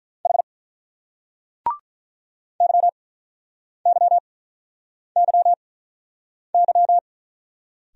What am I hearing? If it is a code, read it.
Morse code, heard as a keyed tone, about 48 wpm, spaced as for 9 wpm, 700 Hz. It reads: H YYYY